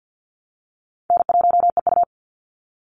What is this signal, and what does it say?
Morse code, heard as a keyed tone, about 48 wpm, 700 Hz. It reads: N1EU
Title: NA SOTA Chaser Callsigns 50wpm